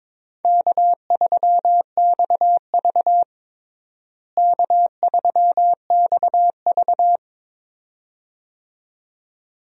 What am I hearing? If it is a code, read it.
Morse code, heard as a keyed tone, about 22 wpm, 700 Hz. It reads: K3XV K3XV
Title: USA Call Signs - 4 Characters - Repeated 2x - 22wpm